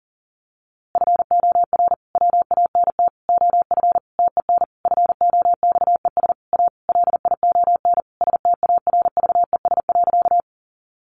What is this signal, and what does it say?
Morse code, heard as a keyed tone, about 40 wpm, 700 Hz. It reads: FOR WANT OF TEN FOXES A LION STARVES.
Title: A Sound Of Thunder 40wpm